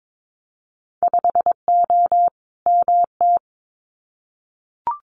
Morse code, heard as a keyed tone, about 22 wpm, 700 Hz. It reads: 5O MT
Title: Winter Field Day Exchanges Encore 22wpm